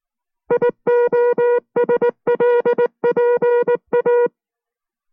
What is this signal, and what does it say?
Morse code, heard as a keyed tone, about 19 wpm, 480 Hz. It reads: IOSLPA